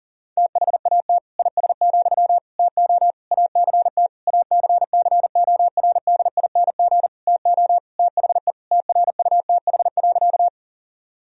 Morse code, heard as a keyed tone, about 40 wpm, 700 Hz. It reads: THAT IS, TO ACT ACCORDING TO THE TRUTH.
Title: Sentences from Top 1000 Words - Review Part 4 - 40wpm